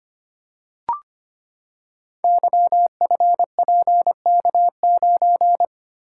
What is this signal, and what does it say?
Morse code, heard as a keyed tone, about 25 wpm, 700 Hz. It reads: YFPK9
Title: Code Groups - 5 Letter:Numbers - 25wpm